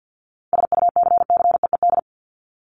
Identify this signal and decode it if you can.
Morse code, heard as a keyed tone, about 48 wpm, 700 Hz. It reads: SUCCEED
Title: Top 1501-1750 Words 50wpm